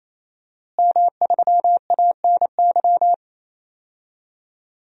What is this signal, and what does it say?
Morse code, heard as a keyed tone, about 28 wpm, 700 Hz. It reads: M3ANY
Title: DXCC-100 Call Signs - 5 Characters - Repeated 3x - 28wpm